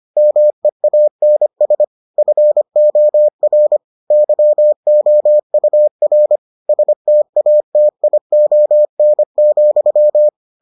Morse code, heard as a keyed tone, about 25 wpm, 600 Hz. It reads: MEANS FOR YOUR STATION,